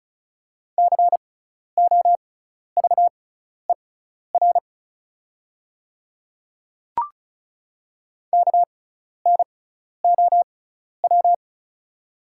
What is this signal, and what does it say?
Morse code, heard as a keyed tone, about 36 wpm, spaced as for 12 wpm, 700 Hz. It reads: COVER KNOW
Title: 6x Character Spacing - Top 300 Words - 35wpm